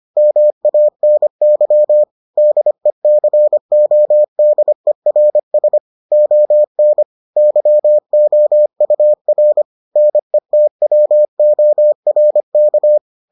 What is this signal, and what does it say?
Morse code, heard as a keyed tone, about 25 wpm, 600 Hz. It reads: MANY DECODERS ON YOUR NETWORK